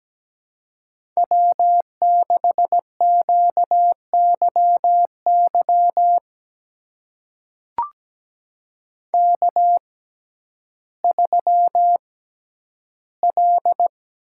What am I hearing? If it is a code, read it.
Morse code, heard as a keyed tone, about 17 wpm, spaced as for 10 wpm, 700 Hz. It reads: W6QYY K3L
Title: USA Call Signs - 5 Characters - 6x Character Spacing - 17wpm